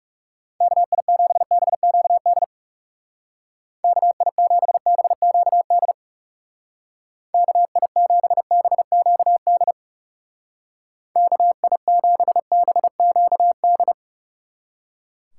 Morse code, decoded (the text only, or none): KI7BQD KI7BQD KI7BQD KI7BQD